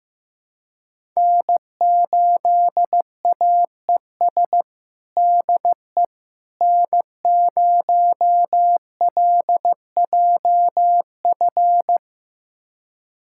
Morse code, heard as a keyed tone, about 15 wpm, 700 Hz. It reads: N8AES DE N0LJF
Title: USA Callsign DE Callsign - 5 Characters - 15wpm